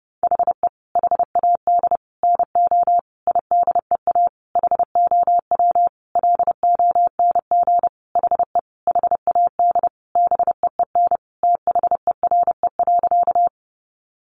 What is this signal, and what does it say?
Morse code, heard as a keyed tone, about 30 wpm, 700 Hz. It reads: HE HAD NO IDEA HOW LONG HE HAD BEEN THERE.
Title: Sentences from Top 300 Words 30wpm